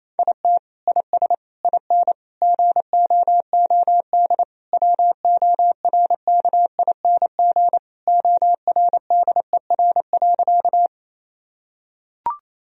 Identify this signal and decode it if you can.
Morse code, heard as a keyed tone, about 28 wpm, 700 Hz. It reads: IT IS IN GOOD WORKING ORDER.
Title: Sentences from Top 1000 Words - Review Part 4 - 28wpm